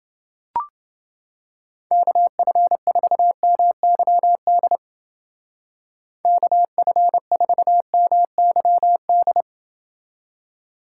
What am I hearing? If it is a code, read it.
Morse code, heard as a keyed tone, about 28 wpm, 700 Hz. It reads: KF4MYD KF4MYD